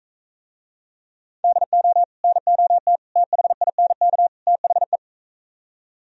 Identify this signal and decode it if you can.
Morse code, heard as a keyed tone, about 43 wpm, 700 Hz. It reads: DO NOT THINK THE